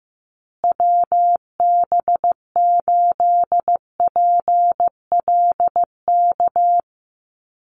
Morse code, heard as a keyed tone, about 15 wpm, 700 Hz. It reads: WB8PLK
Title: USA Call Signs Repeated 2x 15wpm